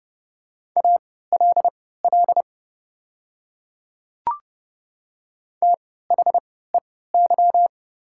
Morse code, heard as a keyed tone, about 30 wpm, spaced as for 17 wpm, 690 Hz. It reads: ALL THEY
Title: 3x Character Spacing - Top 200 Words - 30wpm